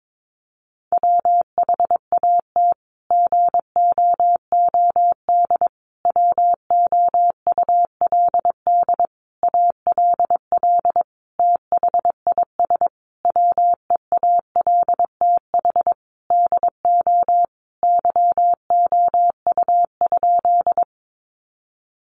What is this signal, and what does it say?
Morse code, heard as a keyed tone, about 22 wpm, 700 Hz. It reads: WHAT GOOD WOULD ALL THIS WEALTH DO YOU?
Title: Sentences from Top 1200 Words 22wpm